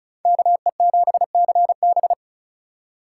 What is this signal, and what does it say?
Morse code, heard as a keyed tone, about 43 wpm, 700 Hz. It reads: KE7CB
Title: Speed Racing - USA Call Signs 35wpm